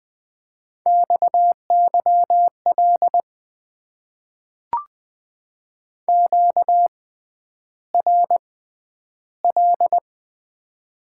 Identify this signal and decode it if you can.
Morse code, heard as a keyed tone, about 20 wpm, spaced as for 10 wpm, 700 Hz. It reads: XYL QRL
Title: Top 90 QSO Elements - 6x Character Spacing - 20wpm